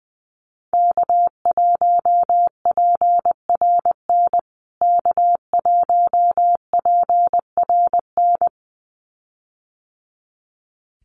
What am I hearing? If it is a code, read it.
Morse code, heard as a keyed tone, about 20 wpm, 700 Hz. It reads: K1PRN K1PRN
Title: USA Call Signs Repeated 2x 20wpm